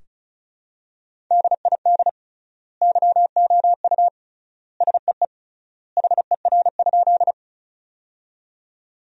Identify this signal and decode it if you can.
Morse code, heard as a keyed tone, about 36 wpm, 700 Hz. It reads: DID YOU SEE HER?
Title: 3x Word-Spacing - Sentences from Top 100 Words - 35wpm